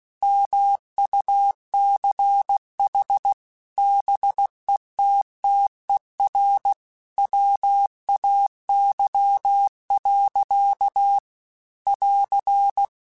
Morse code, heard as a keyed tone, about 16 wpm, 800 Hz. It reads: MUCH BETTER WAY. <AR>
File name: Technical Subjects / Technical Subjects-mp3s / Gates Intro @15